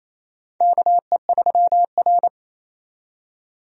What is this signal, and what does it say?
Morse code, heard as a keyed tone, about 28 wpm, 700 Hz. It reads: KE3R
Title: USA Call Signs - 4 Characters - 28wpm